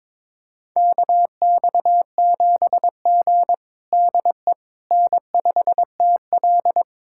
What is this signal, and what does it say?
Morse code, heard as a keyed tone, about 22 wpm, 700 Hz. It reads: KX7G DE N5TL
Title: USA Callsign DE Callsign - 4 Characters - 22wpm